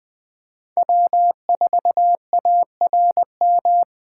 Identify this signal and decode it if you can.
Morse code, heard as a keyed tone, about 20 wpm, 700 Hz. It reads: W4ARM
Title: USA Call Signs - 5 Characters - 4x Character Spacing - 20wpm